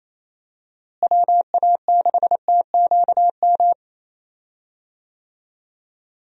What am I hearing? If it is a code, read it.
Morse code, heard as a keyed tone, about 28 wpm, 700 Hz. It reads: WA6TQM
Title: USA Call Signs - 6 Characters - Repeated 2x - 28wpm